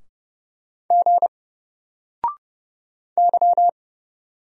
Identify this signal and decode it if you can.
Morse code, heard as a keyed tone, about 30 wpm, 700 Hz. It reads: G Y
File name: Single-Letter-Number-30wpm